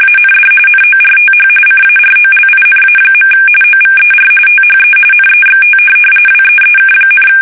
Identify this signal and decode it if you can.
RTTY, about 45.45 baud. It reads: AND NORTH CAROLINA. OVER THE NEXT WEEK,
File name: fsk850 / USS-Massachusetts